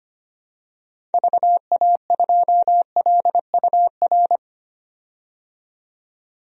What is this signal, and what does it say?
Morse code, heard as a keyed tone, about 25 wpm, 700 Hz. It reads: VA2LUR
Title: DXCC-100 Call Signs - 6 Characters - 25wpm